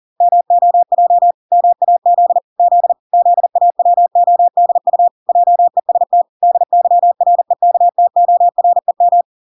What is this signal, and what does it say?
Morse code, heard as a keyed tone, about 40 wpm, 700 Hz. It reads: MOJ MAZ Z ZAWODU JEST DYREKTOREM